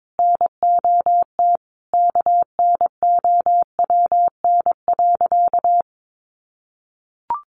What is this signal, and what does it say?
Morse code, heard as a keyed tone, about 22 wpm, 700 Hz. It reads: NOT KNOWN.
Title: Sentences from Top 1300 Words 22wpm